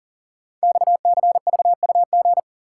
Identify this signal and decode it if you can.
Morse code, heard as a keyed tone, about 40 wpm, 700 Hz. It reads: XCVUG
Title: Code Groups - 5 Letter:Numbers - 40wpm